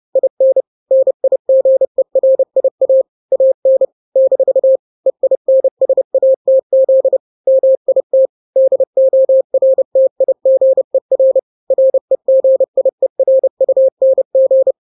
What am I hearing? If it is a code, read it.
Morse code, heard as a keyed tone, about 30 wpm, 530 Hz. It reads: IN NIGERIA AN - EINSATZ MIT DORTIGER REGIERUNG
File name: 29wpm